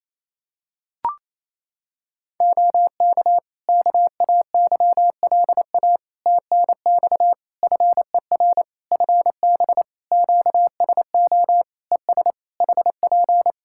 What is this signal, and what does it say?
Morse code, heard as a keyed tone, about 28 wpm, 700 Hz. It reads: OK KAYLA TNX FER FB QSO ES HP
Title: K1USN SST QSO Practice 28wpm